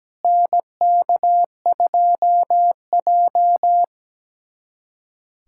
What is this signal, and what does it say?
Morse code, heard as a keyed tone, about 17 wpm, 700 Hz. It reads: NK2J
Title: USA Callsign DE Callsign - 4 Characters - 17wpm